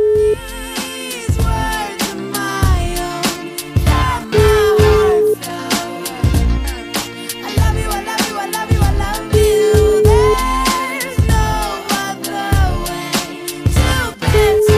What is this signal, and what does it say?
Ringback tone (Europe, 425 Hz), with noise.